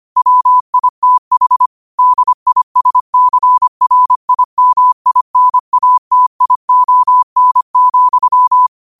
Morse code, heard as a keyed tone, about 25 wpm, 1000 Hz. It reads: WITH DISCRIMINATION,
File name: qotdfiles / 25wpm